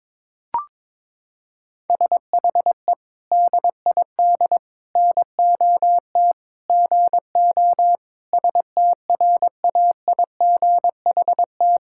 Morse code, heard as a keyed tone, about 22 wpm, 700 Hz. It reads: SHE DID NOT GO STRAIGHT
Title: Sentences from Top 1000 Words - Review Part 6 - 22wpm